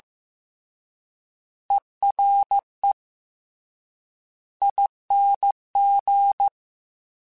Morse code, heard as a keyed tone, about 15 wpm, 800 Hz. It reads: ERE ING
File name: Special INGs / Special INGs-mp3s / Ere ing @13